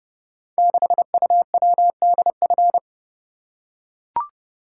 Morse code, heard as a keyed tone, about 30 wpm, 700 Hz. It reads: 6UWDF
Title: Code Groups - 5 Letter:Numbers - 30wpm